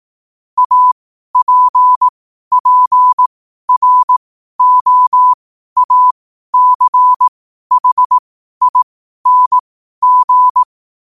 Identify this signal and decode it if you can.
Morse code, heard as a keyed tone, about 18 wpm, spaced as for 12 wpm, 1000 Hz. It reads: APPROACHING
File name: qotdfiles / 12wpm